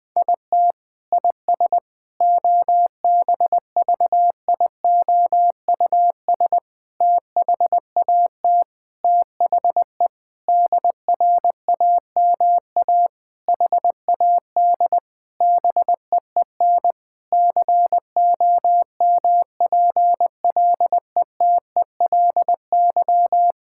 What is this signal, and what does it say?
Morse code, heard as a keyed tone, about 21 wpm, 700 Hz. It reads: IT IS OBVIOUS THAT THE DRAMA HAD BEEN COMPLETELY
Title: Sentences from Top 1600 Words 20wpm